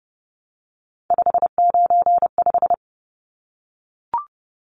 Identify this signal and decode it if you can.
Morse code, heard as a keyed tone, about 30 wpm, 700 Hz. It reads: 595
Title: Numbers - 3-Digit 30wpm